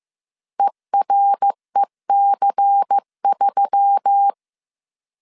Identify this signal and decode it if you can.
Morse code, heard as a keyed tone, about 15 wpm, 800 Hz. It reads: EREC3
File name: CW / DET / 1 6 1500C S C 14wpm 2951